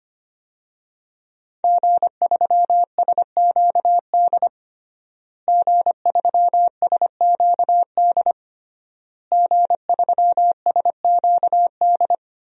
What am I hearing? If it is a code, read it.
Morse code, heard as a keyed tone, about 25 wpm, 700 Hz. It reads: G3SQD G3SQD G3SQD
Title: DXCC-100 Call Signs - 5 Characters - Repeated 3x - 25wpm